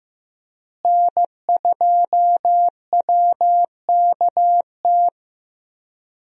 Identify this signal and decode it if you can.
Morse code, heard as a keyed tone, about 16 wpm, 700 Hz. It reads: N2WKT
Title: USA Call Signs - 5 Characters - Repeated 3x - 15wpm